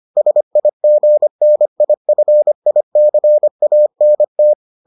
Morse code, heard as a keyed tone, about 25 wpm, 600 Hz. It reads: SIGNIFICANT